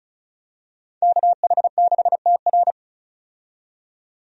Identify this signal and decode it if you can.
Morse code, heard as a keyed tone, about 34 wpm, 700 Hz. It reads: KH6TR
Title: DXCC-100 Call Signs - 5 Characters - 35wpm